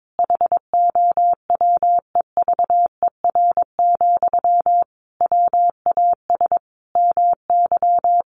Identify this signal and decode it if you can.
Morse code, heard as a keyed tone, about 22 wpm, 700 Hz. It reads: HOWEVER, WAS MY
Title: Sentences from Top 1500 Words 22wpm